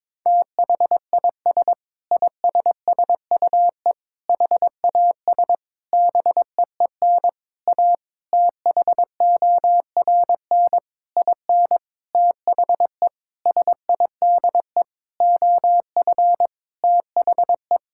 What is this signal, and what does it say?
Morse code, heard as a keyed tone, about 22 wpm, 700 Hz. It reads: THIS ISSUE HAS BEEN A THORN IN THE SIDE OF THE
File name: practice / 22wpm / Sentences with 8-Word Phrases - 22wpm